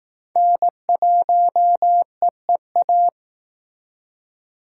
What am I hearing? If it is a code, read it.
Morse code, heard as a keyed tone, about 18 wpm, 700 Hz. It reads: N1EEA
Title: Speed Racing - USA 1x3 Call Signs 15wpm